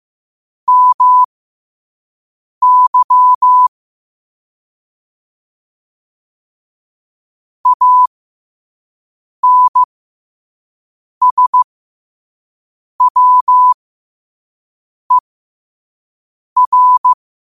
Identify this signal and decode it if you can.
Morse code, heard as a keyed tone, about 15 wpm, spaced as for 5 wpm, 1000 Hz. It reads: MY ANSWER
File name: 5wpm